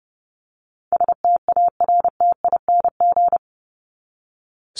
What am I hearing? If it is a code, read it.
Morse code, heard as a keyed tone, about 30 wpm, 700 Hz. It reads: STARTING
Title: ING Suffix - Top 100 Words with Base Words 30wpm